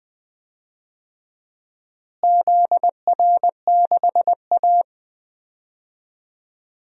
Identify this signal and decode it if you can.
Morse code, heard as a keyed tone, about 20 wpm, 700 Hz. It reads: ZR6A
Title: DXCC-100 Call Signs - 4 Characters - Repeated 2x - 20wpm